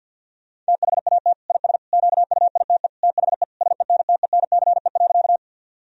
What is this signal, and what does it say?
Morse code, heard as a keyed tone, about 50 wpm, 700 Hz. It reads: THAT IS QUITE THE SENTENCE.
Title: Sentences from Top 1000 Words - Review Part 1 - 50wpm